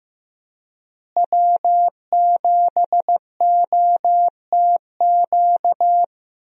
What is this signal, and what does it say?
Morse code, heard as a keyed tone, about 15 wpm, 700 Hz. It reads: W7OTQ
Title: USA Call Signs - 5 Characters - Repeated 2x - 15wpm